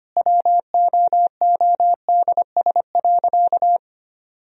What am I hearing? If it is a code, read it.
Morse code, heard as a keyed tone, about 25 wpm, 700 Hz. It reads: WOODS.
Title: Sentences from Top 1000 Words - Review Part 2 - 25wpm